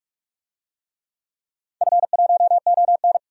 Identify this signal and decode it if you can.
Morse code, heard as a keyed tone, about 46 wpm, 700 Hz. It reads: F1ON
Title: DXCC-100 Call Signs - 4 Characters - 45wpm